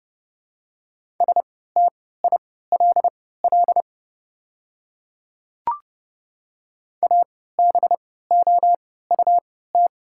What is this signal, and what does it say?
Morse code, heard as a keyed tone, about 30 wpm, spaced as for 17 wpm, 690 Hz. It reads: STILL ABOUT
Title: 3x Character Spacing - Top 300 Words - 30wpm